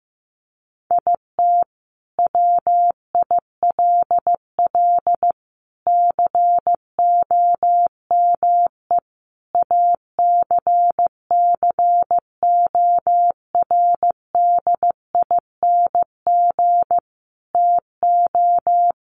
Morse code, heard as a keyed tone, about 15 wpm, 700 Hz. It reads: IT WILL COME ACCORDING TO
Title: Sentences from Top 1400 Words 15wpm